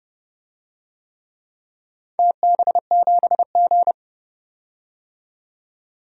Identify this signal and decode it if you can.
Morse code, heard as a keyed tone, about 28 wpm, 700 Hz. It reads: TB7G
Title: DXCC-100 Call Signs - 4 Characters - 30wpm